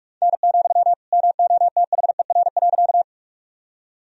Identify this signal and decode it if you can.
Morse code, heard as a keyed tone, about 46 wpm, 700 Hz. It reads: N, MOTHER.
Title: Sentences from Top 200 Words and Abbreviations 45wpm